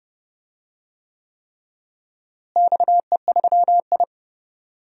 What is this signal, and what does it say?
Morse code, heard as a keyed tone, about 30 wpm, 700 Hz. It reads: XE3I
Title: DXCC-100 Call Signs - 4 Characters - Repeated 3x - 30wpm